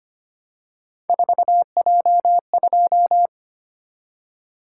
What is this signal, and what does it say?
Morse code, heard as a keyed tone, about 25 wpm, 700 Hz. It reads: 4J2
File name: Three Letter-Number Combinations - 25wpm